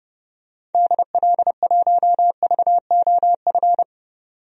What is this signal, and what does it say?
Morse code, heard as a keyed tone, about 30 wpm, 700 Hz. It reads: DL1VOF
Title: DXCC-100 Call Signs - 6 Characters - 30wpm